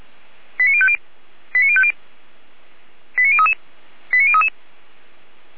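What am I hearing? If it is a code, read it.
Five-tone selective call (ZVEI), heard as quick runs of five tones, tones of about 70 ms. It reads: 89050 89050 89033 89033
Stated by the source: ZVEI five-tone selective call received on an SDR.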